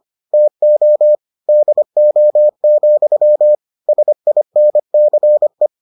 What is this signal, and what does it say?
Morse code, heard as a keyed tone, about 25 wpm, 600 Hz. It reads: TO DO, SINCE